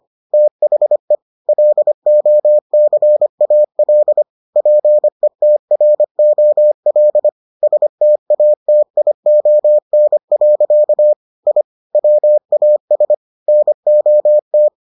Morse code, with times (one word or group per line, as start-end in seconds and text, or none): 0.3-1.1 THE
1.5-4.2 LOCAL
4.6-7.3 PETROL
7.6-11.1 STATION.
11.5-11.6 I
11.9-13.1 WAS
13.5-14.7 NOT